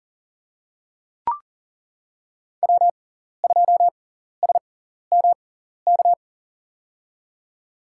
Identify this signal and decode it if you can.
Morse code, heard as a keyed tone, about 40 wpm, spaced as for 14 wpm, 700 Hz. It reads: W2SMK
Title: USA Call Signs - 5 Characters - 6x Character Spacing - 40wpm